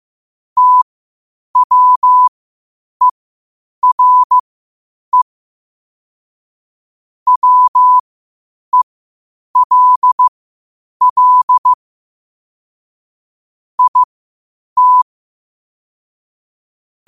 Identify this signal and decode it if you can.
Morse code, heard as a keyed tone, about 15 wpm, spaced as for 8 wpm, 1000 Hz. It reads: TWERE WELL IT
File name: qotdfiles / 8wpm